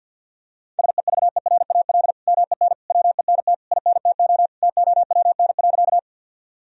Morse code, heard as a keyed tone, about 47 wpm, 700 Hz. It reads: SEVERAL MEN WENT INTO TOWN.
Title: Sentences from Top 1000 Words - Review Part 2 - 50wpm